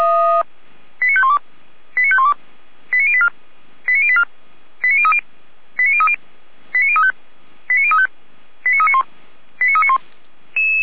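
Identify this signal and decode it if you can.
Five-tone selective call (ZVEI), heard as quick runs of five tones, tones of about 70 ms. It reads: 89512 89512 89084 89084 89030 89030 89035 89035 89391 89391
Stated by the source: ZVEI five-tone selective call received on an SDR.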